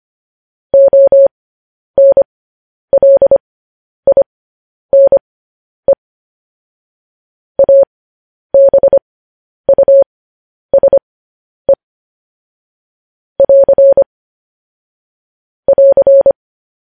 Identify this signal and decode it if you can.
Morse code, heard as a keyed tone, about 25 wpm, spaced as for 10 wpm, 550 Hz. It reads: ONLINE ABUSE <AR> <AR>
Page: news headlines in morse code at 10 wpm